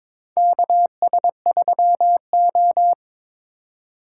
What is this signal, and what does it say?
Morse code, heard as a keyed tone, about 22 wpm, 700 Hz. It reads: KS3O
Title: USA Call Signs - 4x Character Spacing - 22wpm